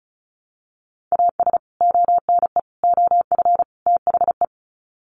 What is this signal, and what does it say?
Morse code, heard as a keyed tone, about 34 wpm, 700 Hz. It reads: AS ONE OF THE
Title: Sets of 4 Words 35wpm